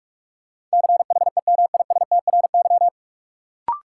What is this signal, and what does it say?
Morse code, heard as a keyed tone, about 46 wpm, 700 Hz. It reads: CHEMISTRY
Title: Sentences from Top 1000 Words - Review Part 6 - New Word Drill - 45wpm